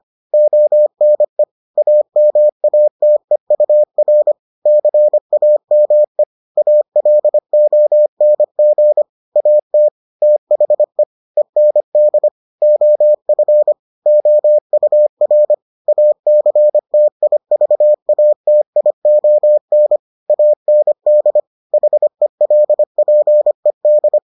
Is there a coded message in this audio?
Yes: Morse code from 0.3 to 24.2 s, heard as a keyed tone, about 25 wpm, 600 Hz. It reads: ONE AMATEUR CAME ALONG AT THE END OF OUR ACTIVATION AND HELPED